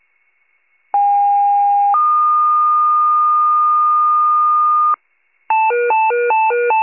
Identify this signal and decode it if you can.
Two-tone page, heard as a one-second tone, then a three-second tone, about 810 then 1220 Hz. It is heard from 0.9 to 5.0 s.
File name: pager / 21 RESCUE / wcalert